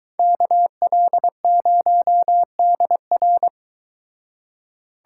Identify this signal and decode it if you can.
Morse code, heard as a keyed tone, about 23 wpm, 700 Hz. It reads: KL0DR